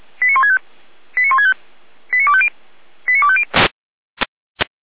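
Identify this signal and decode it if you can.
Five-tone selective call (ZVEI), heard as quick runs of five tones, tones of about 70 ms. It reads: 89165 89165 89250 89250
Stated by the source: ZVEI five-tone selective call received by SDR, from the fire brigade at feuerwehr-orlamuende.org.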